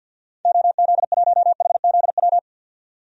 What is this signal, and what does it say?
Morse code, heard as a keyed tone, about 51 wpm, 700 Hz. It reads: OZ1HZW
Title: DXCC-100 Call Signs - 6 Characters - 50wpm